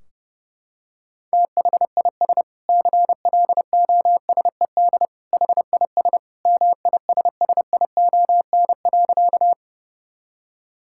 Morse code, heard as a keyed tone, about 32 wpm, 700 Hz. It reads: THIS CLOSED HIS MISSION.
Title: Sentences from Top 1600 Words 30wpm